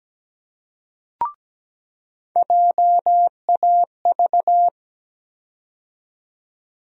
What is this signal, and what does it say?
Morse code, heard as a keyed tone, about 17 wpm, 700 Hz. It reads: JAV
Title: USA Call Sign Postfixes - 3 Letters 17wpm